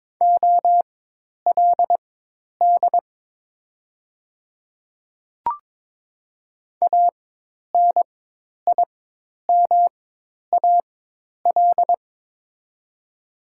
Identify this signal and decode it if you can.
Morse code, heard as a keyed tone, about 22 wpm, spaced as for 10 wpm, 700 Hz. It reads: OLD ANIMAL
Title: 4x Character Spacing - Top 200 Words - 22wpm